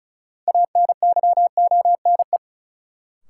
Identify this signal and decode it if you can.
Morse code, heard as a keyed tone, about 34 wpm, 700 Hz. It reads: ANYONE
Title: Sets of 5 Words Encore - New Word Drill - 35wpm